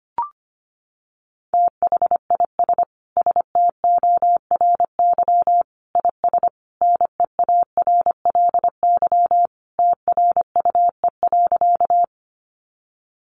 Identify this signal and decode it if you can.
Morse code, heard as a keyed tone, about 25 wpm, 700 Hz. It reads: THIS STORY IS NEARLY TRUE.
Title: Sentences from Top 1300 Words 25wpm